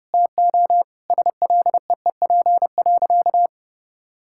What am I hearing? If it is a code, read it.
Morse code, heard as a keyed tone, about 32 wpm, 700 Hz. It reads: TO SLEEP.
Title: Sentences from Top 1000 Words - Review Part 6 - 30wpm